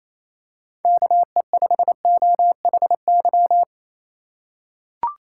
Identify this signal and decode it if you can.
Morse code, heard as a keyed tone, about 27 wpm, 700 Hz. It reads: KE5OHY
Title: USA Call Signs - 6 Characters - 2x Character Spacing - 28wpm